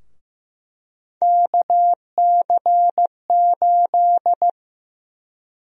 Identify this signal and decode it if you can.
Morse code, heard as a keyed tone, about 15 wpm, 700 Hz. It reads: KC8
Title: USA Call Sign Prefixes - Repeated 2x - 15wpm